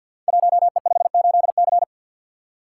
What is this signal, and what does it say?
Morse code, heard as a keyed tone, about 52 wpm, 700 Hz. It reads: 1E58C
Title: Code Groups - 5 Letter:Numbers - 50wpm